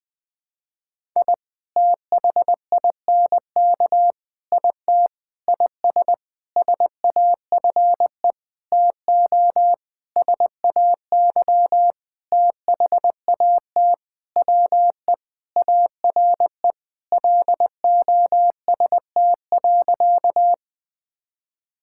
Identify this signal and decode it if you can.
Morse code, heard as a keyed tone, about 21 wpm, 700 Hz. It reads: I THINK IT IS SAFE TO SAY THAT WE ARE LOST.
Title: Sentences with 7-Word Phrases - Spoken Beginning - 20wpm